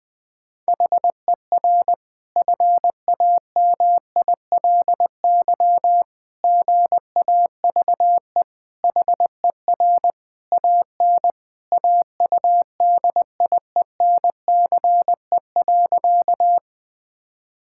Morse code, heard as a keyed tone, about 21 wpm, 700 Hz. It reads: HER FAMILY GAVE HER AN AUDIENCE.
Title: Sentences from Top 1000 Words - Review Part 5 - 20wpm